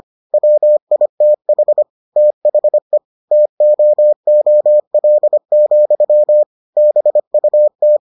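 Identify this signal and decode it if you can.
Morse code, heard as a keyed tone, about 25 wpm, 600 Hz. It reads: WITH THE TOOL, BUT
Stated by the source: Morse edition of the Foundations of Amateur Radio episodes.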